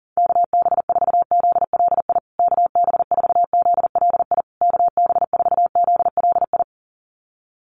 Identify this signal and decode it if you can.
Morse code, heard as a keyed tone, about 40 wpm, 700 Hz. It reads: KB4ZLI KB4ZLI KB4ZLI
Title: USA Call Signs Repeated 3x 40wpm